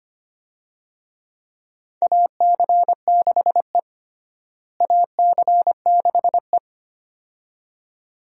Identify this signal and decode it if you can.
Morse code, heard as a keyed tone, about 25 wpm, 700 Hz. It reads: AC6E AC6E
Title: USA Call Signs - 4 Characters - Repeated 2x - 25wpm